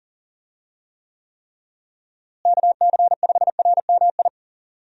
Morse code, heard as a keyed tone, about 39 wpm, 700 Hz. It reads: KC5RMI
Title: USA Call Signs - 6 Characters - Repeated 3x - 40wpm